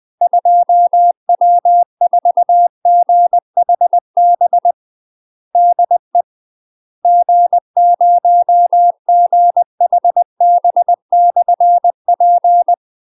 Morse code, heard as a keyed tone, about 20 wpm, 700 Hz. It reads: 2W4GHB DE G0GHB/P
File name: UK MASTER QSOs TRAINING / 20WPM / UK MASTER QSOs 0018